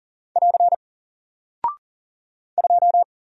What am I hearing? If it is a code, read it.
Morse code, heard as a keyed tone, about 40 wpm, 700 Hz. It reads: <AR> 2
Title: Single Character (Letter:Number:Punctuation:Prosign) Alt 40wpm